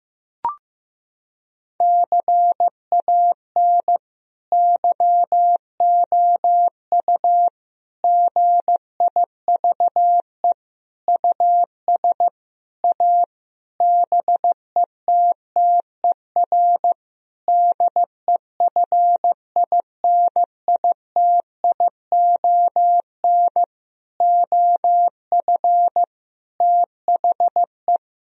Morse code, heard as a keyed tone, about 15 wpm, 700 Hz. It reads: CAN YOU GIVE US A BETTER DEFINITION OF THE
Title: Sentences from Top 1000 Words - Review Part 4 - 15wpm